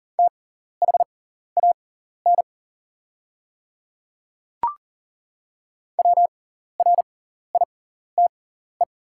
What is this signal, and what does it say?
Morse code, heard as a keyed tone, about 40 wpm, spaced as for 14 wpm, 700 Hz. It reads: THAN WRITE
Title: 6x Character Spacing - Top 100 Words - 40wpm